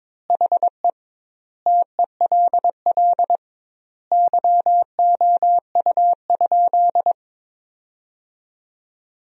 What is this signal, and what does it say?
Morse code, heard as a keyed tone, about 22 wpm, 700 Hz. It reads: HE TELL YOU?
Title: 2x Word-Spacing - Sentences from Top 200 Words - 22wpm